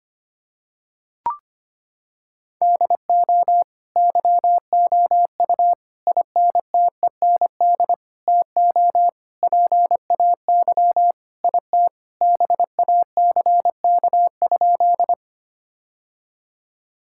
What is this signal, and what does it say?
Morse code, heard as a keyed tone, about 25 wpm, 700 Hz. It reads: DO YOU INTEND TO PAY IT BACK?
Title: Sentences from Top 1600 Words 25wpm